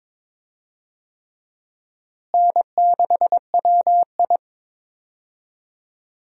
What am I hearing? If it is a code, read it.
Morse code, heard as a keyed tone, about 22 wpm, 700 Hz. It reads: N6WI
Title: USA Call Signs - 4 Characters - Repeated 3x - 22wpm